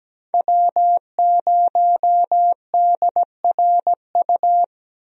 Morse code, heard as a keyed tone, about 17 wpm, 700 Hz. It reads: W0DRU
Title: USA Call Signs - 5 Characters - 17wpm